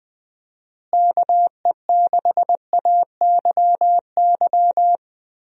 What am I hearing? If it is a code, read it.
Morse code, heard as a keyed tone, about 20 wpm, 700 Hz. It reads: KE6AYY
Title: USA Call Signs - 6 Characters - Repeated 2x - 20wpm